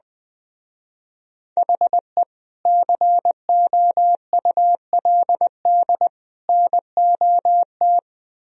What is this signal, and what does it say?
Morse code, heard as a keyed tone, about 20 wpm, 700 Hz. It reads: HE COULD NOT
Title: Speed-Racing - Sets of 3 Words 20wpm